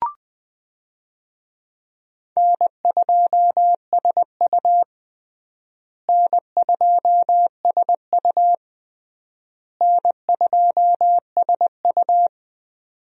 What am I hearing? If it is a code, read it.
Morse code, heard as a keyed tone, about 20 wpm, 700 Hz. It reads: N2SU N2SU N2SU
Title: USA Call Signs - 4 Characters - Repeated 3x - 20wpm